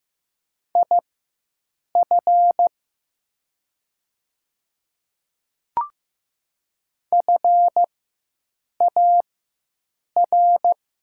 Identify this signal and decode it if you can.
Morse code, heard as a keyed tone, about 15 wpm, spaced as for 7 wpm, 700 Hz. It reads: IF FAR
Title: 4x Character Spacing - Top 300 Words - 15wpm